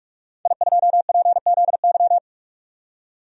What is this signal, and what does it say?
Morse code, heard as a keyed tone, about 46 wpm, 700 Hz. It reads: I2PZY
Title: DXCC-100 Call Signs - 5 Characters - 45wpm